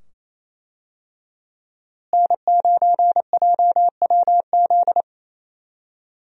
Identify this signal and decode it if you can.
Morse code, heard as a keyed tone, about 28 wpm, 700 Hz. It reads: N9JWZ
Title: USA Call Signs - 5 Characters - Repeated 2x - 28wpm